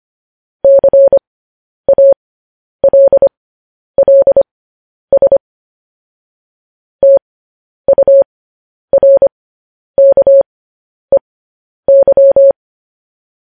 Morse code, heard as a keyed tone, about 25 wpm, spaced as for 10 wpm, 550 Hz. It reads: CALLS TURKEY
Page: news-headlines-in-morse-code-at-10-wpm